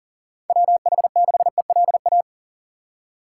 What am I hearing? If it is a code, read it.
Morse code, heard as a keyed tone, about 40 wpm, 700 Hz. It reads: WH6ELA
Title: USA Call Signs - 6 Characters - 2x Character Spacing - 40wpm